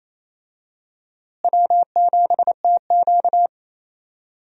Morse code, heard as a keyed tone, about 28 wpm, 700 Hz. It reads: W7TQ
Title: USA Call Signs - 4 Characters - Repeated 3x - 28wpm